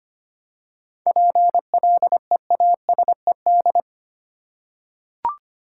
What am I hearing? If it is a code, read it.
Morse code, heard as a keyed tone, about 25 wpm, 700 Hz. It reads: PLEASED
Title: Seven Letter Words - 25wpm